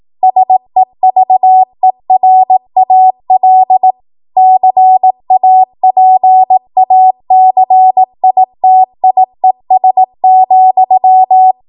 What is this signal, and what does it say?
Morse code, heard as a keyed tone, about 18 wpm, 750 Hz. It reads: SEVERAL CAPACITIES,